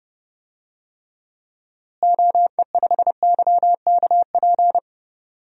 Morse code, heard as a keyed tone, about 29 wpm, 700 Hz. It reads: OE5YKP
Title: DXCC-100 Call Signs - 6 Characters - Repeated 2x - 30wpm